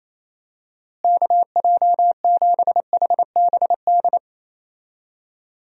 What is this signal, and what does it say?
Morse code, heard as a keyed tone, about 28 wpm, 700 Hz. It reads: KJ7HBD
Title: USA Call Signs - 6 Characters - Repeated 2x - 28wpm